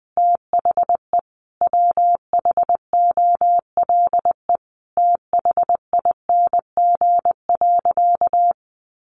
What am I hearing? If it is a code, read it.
Morse code, heard as a keyed tone, about 20 wpm, 700 Hz. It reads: THE WHOLE THING.
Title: Sentences from Top 400 Words 20wpm